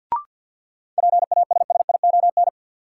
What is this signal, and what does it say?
Morse code, heard as a keyed tone, about 46 wpm, 700 Hz. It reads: PASSION
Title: Sentences from Top 1300 Words - New Word Drill - 50wpm